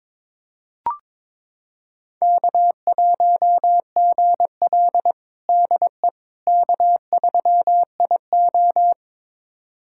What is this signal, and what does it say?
Morse code, heard as a keyed tone, about 22 wpm, 700 Hz. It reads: K1GL DE K3IO
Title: USA Callsign DE Callsign - 4 Characters - 22wpm